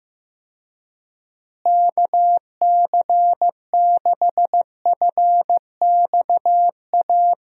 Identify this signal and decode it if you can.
Morse code, heard as a keyed tone, about 16 wpm, 700 Hz. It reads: KC6FXA
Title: USA Call Signs - 6 Characters - Repeated 2x - 15wpm